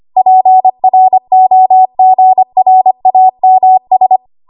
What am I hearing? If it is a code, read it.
Morse code, heard as a keyed tone, about 25 wpm, 750 Hz. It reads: PROGRAMS
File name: Archive / 25 WPM / 25WPM